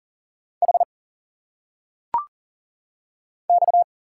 Morse code, heard as a keyed tone, about 40 wpm, 700 Hz. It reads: H X
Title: Single Letters - Phonetic Pronunciation 40wpm